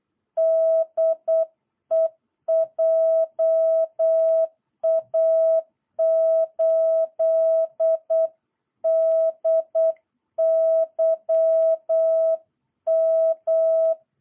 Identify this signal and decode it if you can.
Morse code, heard as a keyed tone, about 8 wpm, 660 Hz. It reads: DE JA8DYM
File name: jh8cbh / 2022cw